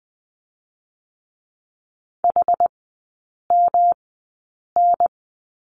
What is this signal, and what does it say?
Morse code, heard as a keyed tone, about 20 wpm, spaced as for 8 wpm, 700 Hz. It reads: HMN
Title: 138 Lesson 23e - Letter f - Three Characters 20wpm